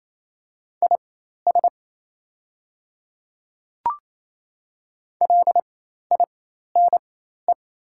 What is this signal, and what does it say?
Morse code, heard as a keyed tone, about 27 wpm, spaced as for 13 wpm, 690 Hz. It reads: IS LINE
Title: 4x Character Spacing - Top 300 Words - 28wpm